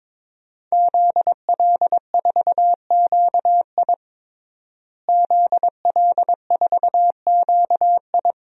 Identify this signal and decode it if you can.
Morse code, heard as a keyed tone, about 21 wpm, 700 Hz. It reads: ZL4QI ZL4QI
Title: DXCC-100 Call Signs - 5 Characters - Repeated 3x - 22wpm